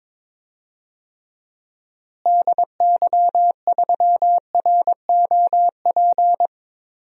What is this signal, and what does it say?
Morse code, heard as a keyed tone, about 23 wpm, 700 Hz. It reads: DY3ROP
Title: DXCC-100 Call Signs - 6 Characters - Repeated 2x - 22wpm